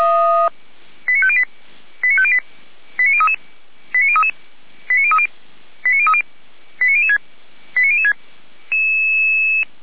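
Five-tone selective call (ZVEI), heard as quick runs of five tones, tones of about 70 ms. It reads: 89508 89508 89033 89033 89030 89030 89006 89006
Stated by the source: ZVEI five-tone call received on SDR.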